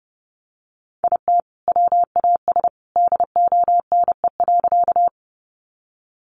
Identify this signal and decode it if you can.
Morse code, heard as a keyed tone, about 30 wpm, 700 Hz. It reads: IT WAS DONE.
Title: Sentences from Top 500 Words 30wpm